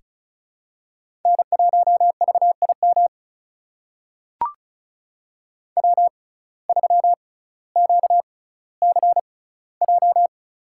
Morse code, heard as a keyed tone, about 34 wpm, spaced as for 18 wpm, 700 Hz. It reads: N1VIM W3QCJ
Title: USA Call Signs - 5 Characters - 6x Character Spacing - 35wpm